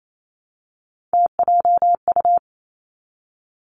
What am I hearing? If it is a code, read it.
Morse code, heard as a keyed tone, about 28 wpm, 690 Hz. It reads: TJU
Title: Three Challenging Letter-Number Combinations 28wpm